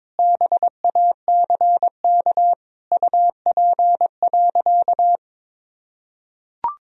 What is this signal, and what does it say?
Morse code, heard as a keyed tone, about 22 wpm, 700 Hz. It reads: BACK UP.
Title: Sentences from Top 1000 Words - Review Part 2 - 22wpm